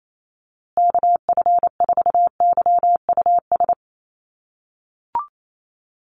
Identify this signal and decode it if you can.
Morse code, heard as a keyed tone, about 28 wpm, 700 Hz. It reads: KF4YUS
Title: USA Call Signs Repeated 3x 28wpm